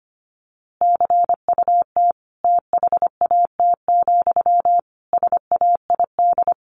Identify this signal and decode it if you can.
Morse code, heard as a keyed tone, about 25 wpm, 700 Hz. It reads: CUT THAT, SAID
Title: A Sound Of Thunder 25wpm